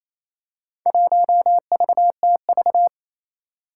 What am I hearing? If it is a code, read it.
Morse code, heard as a keyed tone, about 29 wpm, 700 Hz. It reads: 1VTV
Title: Four Challenging Letter-Number Combinations 28wpm